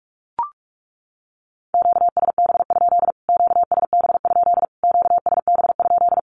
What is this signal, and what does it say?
Morse code, heard as a keyed tone, about 44 wpm, 700 Hz. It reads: QSB? QSB? QSB?
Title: Q-Codes Repeated 3x 45wpm